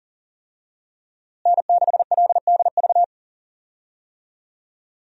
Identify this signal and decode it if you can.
Morse code, heard as a keyed tone, about 39 wpm, 700 Hz. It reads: N6LDV
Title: USA Call Signs - 5 Characters - Repeated 2x - 40wpm